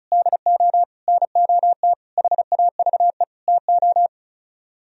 Morse code, heard as a keyed tone, about 44 wpm, 700 Hz. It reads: DO NOT HAVE TO